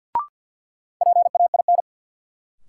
Morse code, heard as a keyed tone, about 51 wpm, 700 Hz. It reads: PAIN
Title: Sentences from Top 1600 Words - New Word Drill - 50wpm